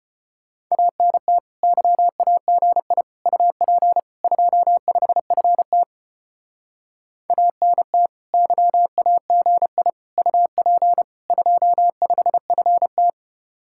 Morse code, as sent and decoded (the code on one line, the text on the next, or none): .- -. - / -.-- .- --. .. / ..- .--. / ..--- ..... ..-. - / .- -. - / -.-- .- --. .. / ..- .--. / ..--- ..... ..-. -
ANT YAGI UP 25FT ANT YAGI UP 25FT